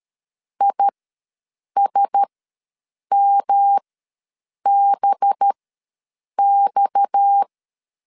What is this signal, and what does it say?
Morse code, heard as a keyed tone, about 13 wpm, spaced as for 7 wpm, 800 Hz. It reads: ISMBX
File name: LEZIONI / 7 GF9Z / 7 G ws 12wpm 3140